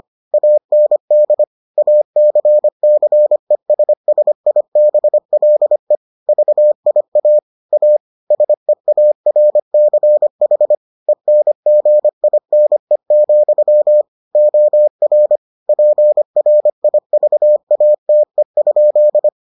Morse code, heard as a keyed tone, about 25 wpm, 600 Hz. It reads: AND ACCESSIBLE VIA A SEARCH ENGINE, OR PRIVATE?